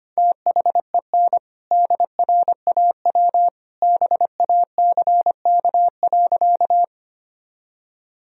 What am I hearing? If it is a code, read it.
Morse code, heard as a keyed tone, about 25 wpm, 700 Hz. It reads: THEN DRAW BACK.
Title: Sentences from Top 1000 Words - Review Part 1 - 25wpm